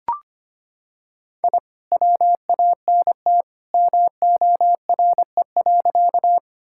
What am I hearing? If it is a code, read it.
Morse code, heard as a keyed tone, about 25 wpm, 700 Hz. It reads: I WANT MORE.
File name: practice / 25wpm / Sentences with 2-Word Phrases - 25wpm